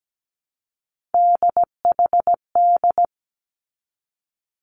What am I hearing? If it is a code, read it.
Morse code, heard as a keyed tone, about 17 wpm, 700 Hz. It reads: DHD
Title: Three Challenging Letter-Number Combinations 17wpm